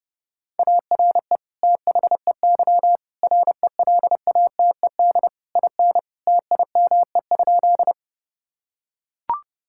Morse code, heard as a keyed tone, about 32 wpm, 700 Hz. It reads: ARE THEY RELATED IN TIME?
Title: Sentences from Top 1000 Words - Review Part 5 - 30wpm